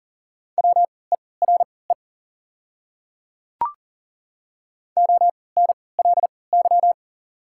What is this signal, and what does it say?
Morse code, heard as a keyed tone, about 40 wpm, spaced as for 23 wpm, 700 Hz. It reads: WERE ONLY
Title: 3x Character Spacing - Top 500 Words - 40wpm